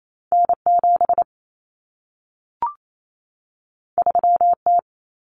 Morse code, heard as a keyed tone, about 28 wpm, 700 Hz. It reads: N7 3T
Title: Common Cut Numbers - 2-Digit 28wpm